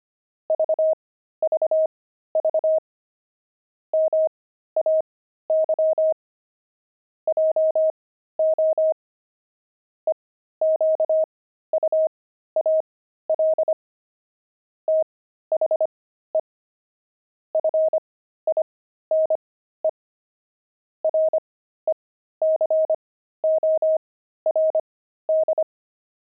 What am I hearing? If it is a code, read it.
Morse code, heard as a keyed tone, about 25 wpm, spaced as for 13 wpm, 630 Hz. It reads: VVV MAY JO EQUAL THE FINE RECORD
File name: pangram-fs-13-cs-25-puzzles